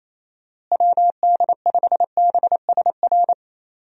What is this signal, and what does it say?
Morse code, heard as a keyed tone, about 28 wpm, 700 Hz. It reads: WD5BSR
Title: USA Call Signs - 6 Characters - Repeated 3x - 28wpm